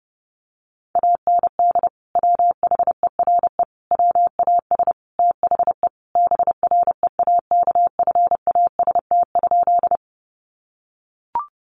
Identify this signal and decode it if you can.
Morse code, heard as a keyed tone, about 30 wpm, 700 Hz. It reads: AND WHERE WAS THE BREAKFAST?
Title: Sentences from Top 1500 Words 30wpm